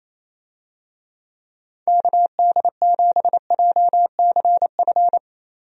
Morse code, heard as a keyed tone, about 28 wpm, 700 Hz. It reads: KD7JCF
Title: USA Call Signs - 6 Characters - Repeated 2x - 28wpm